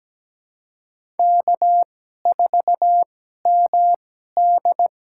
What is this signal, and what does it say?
Morse code, heard as a keyed tone, about 17 wpm, spaced as for 12 wpm, 700 Hz. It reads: K4MD
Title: USA Call Signs - 2x Character Spacing - 17wpm